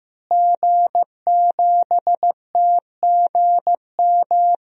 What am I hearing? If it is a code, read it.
Morse code, heard as a keyed tone, about 15 wpm, 700 Hz. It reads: G7TGM
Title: DXCC-100 Call Signs - 5 Characters - Repeated 2x - 15wpm